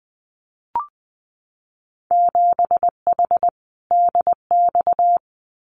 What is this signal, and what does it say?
Morse code, heard as a keyed tone, about 20 wpm, 700 Hz. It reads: 7H DX
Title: Winter Field Day Exchanges Encore 20wpm